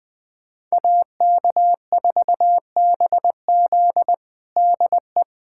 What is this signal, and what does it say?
Morse code, heard as a keyed tone, about 20 wpm, 700 Hz. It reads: AK4BZ DE
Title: USA Callsign DE Callsign - 5 Characters - 20wpm